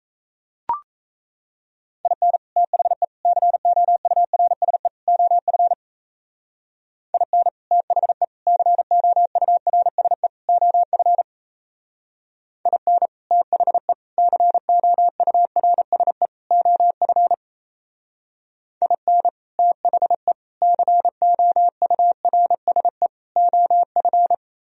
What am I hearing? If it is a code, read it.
Morse code, heard as a keyed tone, about 37 wpm, 700 Hz. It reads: IN THE COURSE OF IN THE COURSE OF IN THE COURSE OF IN THE COURSE OF